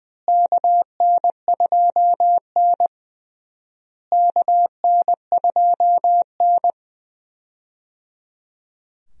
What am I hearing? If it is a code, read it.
Morse code, heard as a keyed tone, about 21 wpm, 700 Hz. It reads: KN2N KN2N
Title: USA Call Signs - 4 Characters - Repeated 3x - 20wpm